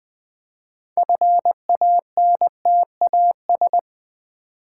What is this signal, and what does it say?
Morse code, heard as a keyed tone, about 20 wpm, 700 Hz. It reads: FANTAS
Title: Build A Word - 3 Letters At A Time - 9-Letter Words - 20wpm